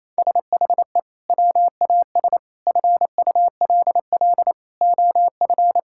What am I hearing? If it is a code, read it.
Morse code, heard as a keyed tone, about 28 wpm, 700 Hz. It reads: SHE WAS FULL OF
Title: Sentences from Top 1000 Words - Review Part 6 - 28wpm